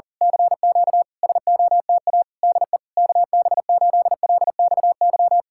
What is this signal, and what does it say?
Morse code, heard as a keyed tone, about 37 wpm, 700 Hz. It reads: CQ SOTA DE KB8LXY
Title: Speed-Racing Common QSO Phrases Encore - 40wpm